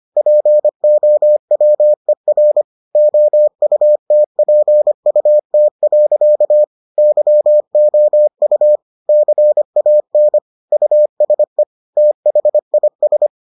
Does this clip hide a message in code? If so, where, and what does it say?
Yes: Morse code from 0.2 to 13.3 s, heard as a keyed tone, about 25 wpm, 600 Hz. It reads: POWER OUTPUT. YOU CAN USE THIS